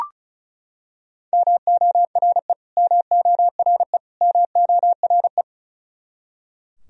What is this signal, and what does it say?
Morse code, heard as a keyed tone, about 34 wpm, 700 Hz. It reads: MORE MORE MORE
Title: Top 100 Words Repeated 35wpm